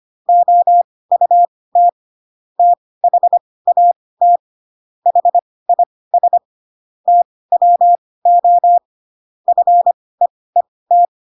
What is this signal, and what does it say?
Morse code, heard as a keyed tone, about 25 wpm, spaced as for 18 wpm, 700 Hz. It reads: OUT THAT HIS TWO FEET
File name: Pinocchio / Pinocchio ch7 12 25 ik1hge